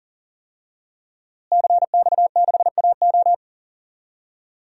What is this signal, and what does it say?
Morse code, heard as a keyed tone, about 37 wpm, 700 Hz. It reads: CX6AO
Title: DXCC-100 Call Signs - 5 Characters - Repeated 3x - 40wpm